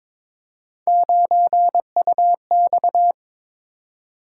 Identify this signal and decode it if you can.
Morse code, heard as a keyed tone, about 21 wpm, 700 Hz. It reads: 9UX
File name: practice / 22wpm / Three Letter-Number Combinations - 22wpm